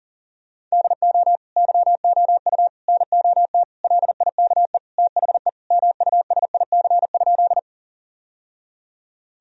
Morse code, heard as a keyed tone, about 40 wpm, 700 Hz. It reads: DO YOU NOT LIKE THE MUSIC?
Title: Sentences from Top 1000 Words - Review Part 1 - 40wpm